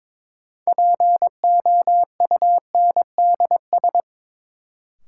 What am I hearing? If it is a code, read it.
Morse code, heard as a keyed tone, about 22 wpm, 700 Hz. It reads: POUNDS
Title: Sentences from Top 1000 Words - Review Part 2 - New Word Drill - 22wpm